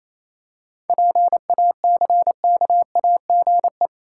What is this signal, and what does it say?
Morse code, heard as a keyed tone, about 28 wpm, 700 Hz. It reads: PACKAGE
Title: Seven Letter Words - 28wpm